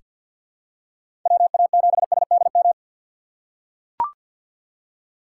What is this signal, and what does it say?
Morse code, heard as a keyed tone, about 45 wpm, 700 Hz. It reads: WA7SDM
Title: USA Call Signs - 6 Characters - 3x Character Spacing - 50wpm